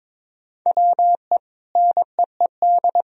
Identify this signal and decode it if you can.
Morse code, heard as a keyed tone, about 22 wpm, 700 Hz. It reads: WE NEED
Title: Sentences from Top 1000 Words - Review Part 4 - 22wpm